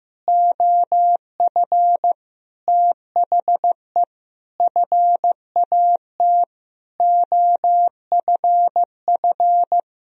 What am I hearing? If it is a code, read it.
Morse code, heard as a keyed tone, about 15 wpm, 700 Hz. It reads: OF THE FAT OFF
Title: Sentences from Top 1000 Words - Review Part 4 - 15wpm